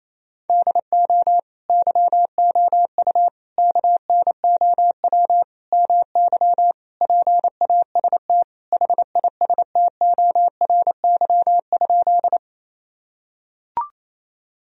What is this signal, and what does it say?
Morse code, heard as a keyed tone, about 27 wpm, 700 Hz. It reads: DO YOU KNOW MY PAST HISTORY?
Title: Sentences from Top 1000 Words - Review Part 3 - 28wpm